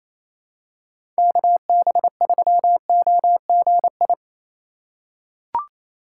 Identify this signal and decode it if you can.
Morse code, heard as a keyed tone, about 28 wpm, 700 Hz. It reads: KB3OGI
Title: USA Call Signs - 4x Character Spacing - 28wpm